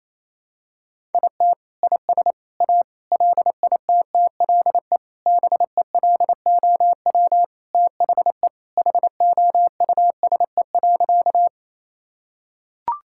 Morse code, heard as a keyed tone, about 28 wpm, 700 Hz. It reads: IT IS A LITTLE BELOW THE HOUSE.
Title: Sentences from Top 1000 Words - Review Part 5 - 28wpm